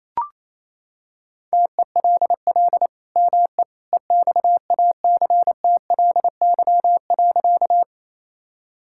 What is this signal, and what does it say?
Morse code, heard as a keyed tone, about 28 wpm, 700 Hz. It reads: TELL ME EXACTLY.
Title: Sentences from Top 1000 Words - Review Part 6 - 28wpm